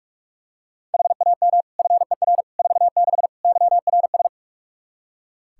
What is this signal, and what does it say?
Morse code, heard as a keyed tone, about 42 wpm, 700 Hz. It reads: HAM FER 46 YRS
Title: Practice QSO - Exchange 3 - More Info - 45wpm